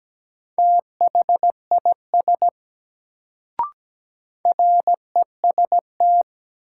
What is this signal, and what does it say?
Morse code, heard as a keyed tone, about 17 wpm, 700 Hz. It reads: THIS REST
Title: Sets of 2 Words - New Word Drill - 17wpm